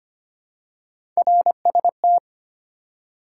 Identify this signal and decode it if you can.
Morse code, heard as a keyed tone, about 25 wpm, 700 Hz. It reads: RST
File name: Top 90 QSO Elements - 2x Character Spacing - 25wpm